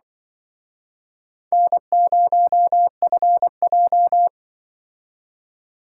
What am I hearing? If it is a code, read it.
Morse code, heard as a keyed tone, about 23 wpm, 700 Hz. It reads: N0FJ
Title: Speed Racing - USA Call Signs 20wpm